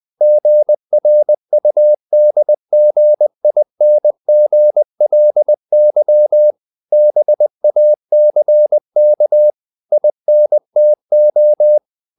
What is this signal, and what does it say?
Morse code, heard as a keyed tone, about 20 wpm, 600 Hz. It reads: GRUDGINGLY BACK INTO
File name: the old man and the sea part 2 20wpm